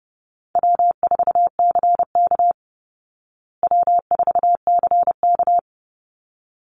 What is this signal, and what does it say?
Morse code, heard as a keyed tone, about 30 wpm, 700 Hz. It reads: W4CK W4CK
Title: Easy Call Signs Repeated 2x - 30wpm